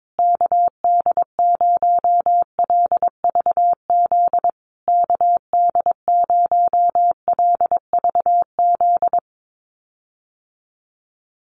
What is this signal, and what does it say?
Morse code, heard as a keyed tone, about 22 wpm, 700 Hz. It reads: KD0LVZ KD0LVZ
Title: USA Call Signs Repeated 3x 22wpm